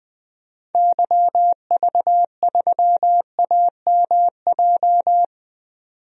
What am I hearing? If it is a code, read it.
Morse code, heard as a keyed tone, about 20 wpm, 700 Hz. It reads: YV3AMJ
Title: DXCC-100 Call Signs - 6 Characters - 20wpm